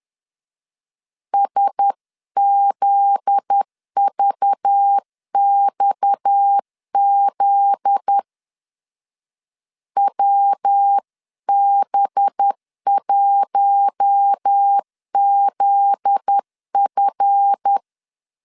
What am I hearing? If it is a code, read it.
Morse code, heard as a keyed tone, about 10 wpm, 800 Hz. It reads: SZVXZ WB1ZF
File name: LEZIONI / 7 GF9Z / 7 GF9Z 10wpm 2611